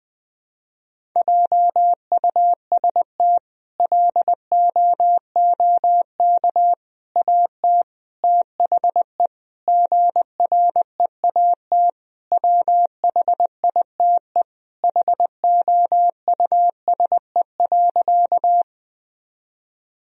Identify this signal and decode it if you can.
Morse code, heard as a keyed tone, about 20 wpm, 700 Hz. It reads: JUST LOOK AT THE GREAT WHITE HOUSE.
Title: Sentences from Top 1000 Words - Review Part 1 - 20wpm